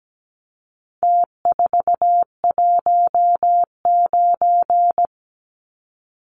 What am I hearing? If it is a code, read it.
Morse code, heard as a keyed tone, about 17 wpm, 700 Hz. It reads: T419
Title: Numbers - Contest Style - 4-Digit 17wpm